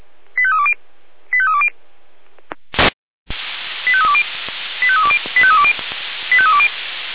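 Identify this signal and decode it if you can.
Five-tone selective call (ZVEI), heard as quick runs of five tones, tones of about 70 ms. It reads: 85320 85320 85320 85320 85320 85320
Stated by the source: ZVEI five-tone call 85320 received on SDR.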